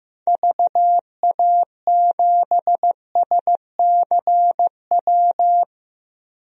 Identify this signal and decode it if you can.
Morse code, heard as a keyed tone, about 15 wpm, 700 Hz. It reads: VA7SCW
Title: DXCC-100 Call Signs - 6 Characters - 15wpm